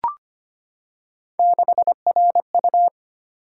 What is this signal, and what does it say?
Morse code, heard as a keyed tone, about 25 wpm, 700 Hz. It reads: 6RU
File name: Three Letter-Number Combinations - 25wpm